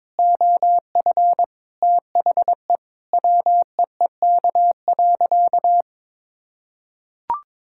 Morse code, heard as a keyed tone, about 22 wpm, 700 Hz. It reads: OF THE WEEK.
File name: practice / 22wpm / Sentences with 4-Word Phrases - 22wpm